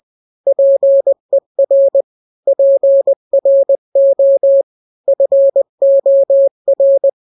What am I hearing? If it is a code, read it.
Morse code, heard as a keyed tone, about 20 wpm, 550 Hz. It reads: PER PRO FOR
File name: n-grams / 3-grams 20 18 WPM